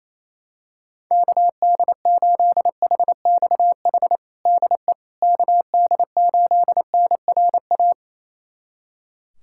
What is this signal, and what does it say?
Morse code, heard as a keyed tone, about 28 wpm, 700 Hz. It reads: KD8HXH DE KD8NRA
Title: USA Callsign DE Callsign 28wpm